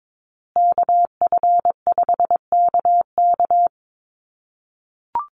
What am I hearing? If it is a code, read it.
Morse code, heard as a keyed tone, about 22 wpm, 700 Hz. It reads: KF5KK
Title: USA Call Signs Repeated 3x 22wpm